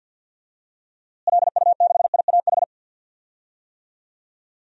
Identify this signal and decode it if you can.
Morse code, heard as a keyed tone, about 52 wpm, 700 Hz. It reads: LU6IAH
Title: DXCC-100 Call Signs - 6 Characters - Repeated 2x - 50wpm